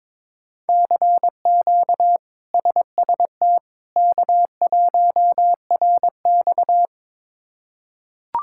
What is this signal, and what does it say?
Morse code, heard as a keyed tone, about 22 wpm, 700 Hz. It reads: CQ SST K1RX
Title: K1USN SST QSO Practice - 22wpm